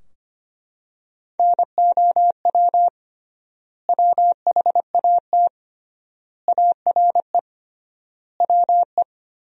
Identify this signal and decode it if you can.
Morse code, heard as a keyed tone, about 25 wpm, 700 Hz. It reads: NOW WHAT ARE WE
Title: 3x Word-Spacing - Sentences from Top 100 Words - 25wpm